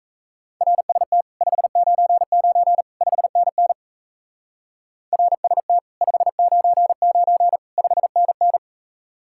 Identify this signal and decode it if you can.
Morse code, heard as a keyed tone, about 36 wpm, 700 Hz. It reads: RST 599 5NN RST 599 5NN